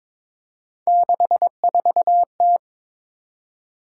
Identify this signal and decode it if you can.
Morse code, heard as a keyed tone, about 22 wpm, 700 Hz. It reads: 64T
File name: practice / 22wpm / Three Letter-Number Combinations - 22wpm